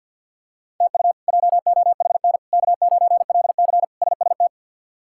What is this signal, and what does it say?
Morse code, heard as a keyed tone, about 52 wpm, 700 Hz. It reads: TU JOHN K9LC SST
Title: K1USN SST QSO Practice - 50wpm